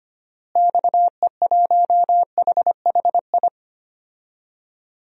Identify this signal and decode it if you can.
Morse code, heard as a keyed tone, about 25 wpm, 700 Hz. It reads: XE1HHI
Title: DXCC-100 Call Signs - 6 Characters - Repeated 3x - 25wpm